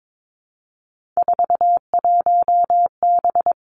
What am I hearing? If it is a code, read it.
Morse code, heard as a keyed tone, about 22 wpm, 700 Hz. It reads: 41B
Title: Three Challenging Letter-Number Combinations 22wpm